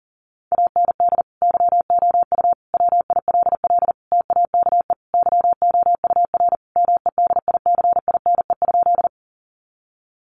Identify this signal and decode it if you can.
Morse code, heard as a keyed tone, about 40 wpm, 700 Hz. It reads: AND YOU WILL TAKE YOUR MEDICINE?
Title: Sentences from Top 900 Words 40wpm